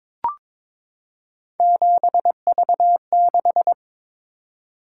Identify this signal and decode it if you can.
Morse code, heard as a keyed tone, about 22 wpm, 700 Hz. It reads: 7V6
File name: practice / 22wpm / Three Letter-Number Combinations - 22wpm